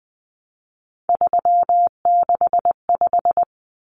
Morse code, heard as a keyed tone, about 20 wpm, 700 Hz. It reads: 365
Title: Common Cut Numbers - 3-Digit 20wpm